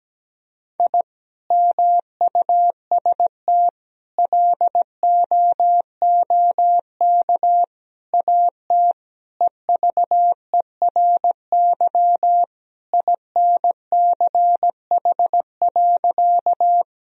Morse code, heard as a keyed tone, about 17 wpm, 700 Hz. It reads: I MUST LOOK AT EVERY INCH.
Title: Sentences from Top 1000 Words - Review Part 3 - 17wpm